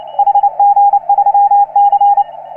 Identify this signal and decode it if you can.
Morse code, heard as a keyed tone, about 29 wpm, 810 Hz. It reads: HG3C
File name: CW / HA3OU RHOBCW REC